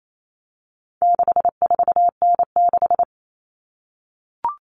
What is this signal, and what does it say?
Morse code, heard as a keyed tone, about 28 wpm, 700 Hz. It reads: 64N6
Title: All Cut Numbers - 4-Digit 28wpm